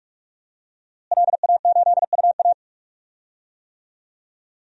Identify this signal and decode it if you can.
Morse code, heard as a keyed tone, about 42 wpm, 700 Hz. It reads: LA8UA
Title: DXCC-100 Call Signs - 5 Characters - Repeated 3x - 45wpm